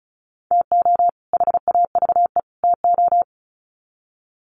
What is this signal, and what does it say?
Morse code, heard as a keyed tone, about 34 wpm, 700 Hz. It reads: TO HAVE TO
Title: Sets of 3 Words 35wpm